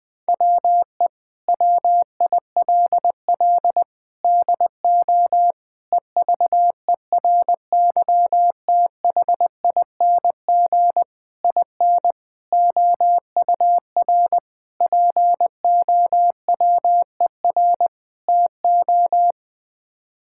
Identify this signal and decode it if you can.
Morse code, heard as a keyed tone, about 19 wpm, 700 Hz. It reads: WE WILL DO EVERYTHING IN OUR POWER TO
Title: Sets of 8 Words - 20wpm